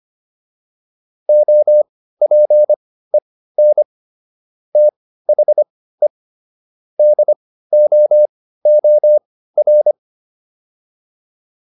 Morse code, heard as a keyed tone, about 25 wpm, spaced as for 15 wpm, 600 Hz. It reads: OPEN THE DOOR
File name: PR305 15